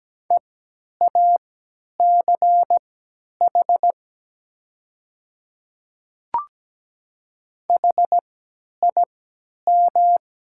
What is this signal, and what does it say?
Morse code, heard as a keyed tone, about 17 wpm, spaced as for 10 wpm, 690 Hz. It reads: EACH HIM
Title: 3x Character Spacing - Top 100 Words - 17wpm